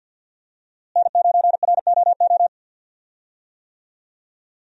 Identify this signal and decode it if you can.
Morse code, heard as a keyed tone, about 48 wpm, 700 Hz. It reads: N9ROO
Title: USA Call Signs - 5 Characters - Repeated 3x - 50wpm